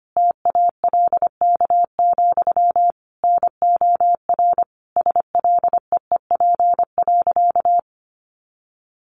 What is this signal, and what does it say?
Morse code, heard as a keyed tone, about 25 wpm, 700 Hz. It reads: TALK, NOR SLEEP.
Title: Sentences from Top 1500 Words 25wpm